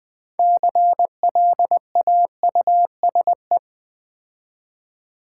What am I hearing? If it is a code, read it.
Morse code, heard as a keyed tone, about 20 wpm, 700 Hz. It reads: CLAUSE
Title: 6 Letter Words - First 3 Letters Given - 20wpm